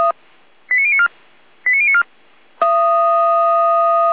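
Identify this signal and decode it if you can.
Five-tone selective call (ZVEI), heard as quick runs of five tones, tones of about 80 ms. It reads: 89094 89094
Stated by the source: ZVEI five-tone call received on SDR, channel 455.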